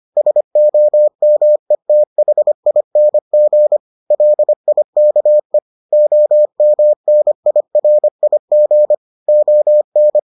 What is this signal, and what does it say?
Morse code, heard as a keyed tone, about 25 wpm, 600 Hz. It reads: SOMETHING LIKE OMNIRIG ON